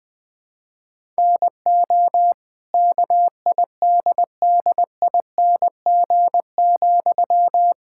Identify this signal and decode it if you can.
Morse code, heard as a keyed tone, about 20 wpm, 700 Hz. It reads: NO KIDDING,
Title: Pangram Challenge v2 - 20wpm